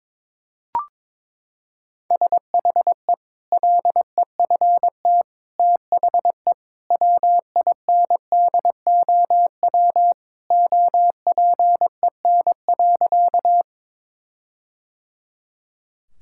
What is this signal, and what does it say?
Morse code, heard as a keyed tone, about 22 wpm, 700 Hz. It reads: SHE LEFT THE WINDOW OPEN.
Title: Sentences from Top 1600 Words 22wpm